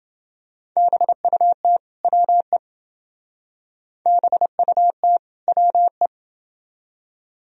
Morse code, heard as a keyed tone, about 27 wpm, 700 Hz. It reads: BUT WE BUT WE